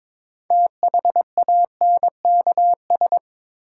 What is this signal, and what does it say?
Morse code, heard as a keyed tone, about 21 wpm, 700 Hz. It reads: THANKS
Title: Sets of 5 Words - New Word Drill - 22wpm